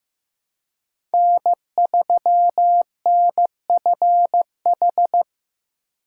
Morse code, heard as a keyed tone, about 15 wpm, 700 Hz. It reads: N3NFH
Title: USA Call Signs - 4x Character Spacing - 15wpm